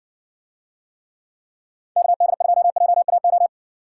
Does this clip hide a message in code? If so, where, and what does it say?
Yes: Morse code from 2.0 to 3.5 s, heard as a keyed tone, about 62 wpm, 700 Hz. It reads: KD2JAO